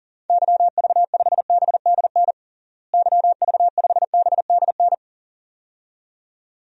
Morse code, heard as a keyed tone, about 40 wpm, 700 Hz. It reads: YV5BDN YV5BDN
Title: DXCC-100 Call Signs - 6 Characters - Repeated 2x - 40wpm